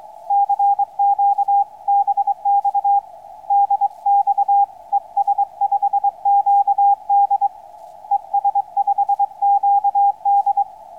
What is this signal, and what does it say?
Morse code, heard as a keyed tone, about 23 wpm, 800 Hz, with radio noise. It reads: CQ BX DX ES5QD ES5QD
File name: es5qd 21-10-13 2048 160m